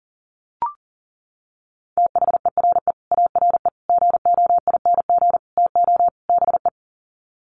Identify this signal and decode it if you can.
Morse code, heard as a keyed tone, about 40 wpm, 700 Hz. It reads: THERE ARE GOING TO BE
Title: Sets of 5 Words 40wpm